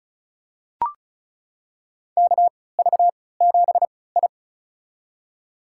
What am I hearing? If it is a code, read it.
Morse code, heard as a keyed tone, about 34 wpm, spaced as for 20 wpm, 700 Hz. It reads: KV7I
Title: USA Call Signs - 3x Character Spacing - 35wpm